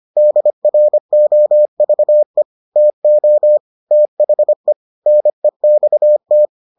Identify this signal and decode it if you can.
Morse code, heard as a keyed tone, about 25 wpm, 600 Hz. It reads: DROVE TO THE NEXT